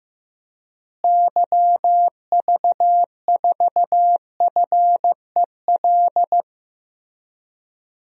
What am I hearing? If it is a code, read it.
Morse code, heard as a keyed tone, about 15 wpm, 700 Hz. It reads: YV4FEL
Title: DXCC-100 Call Signs - 6 Characters - Repeated 3x - 15wpm